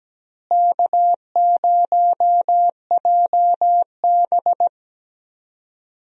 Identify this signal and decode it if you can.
Morse code, heard as a keyed tone, about 17 wpm, 700 Hz. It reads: K0JB
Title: USA Call Signs - 4 Characters - Repeated 3x - 17wpm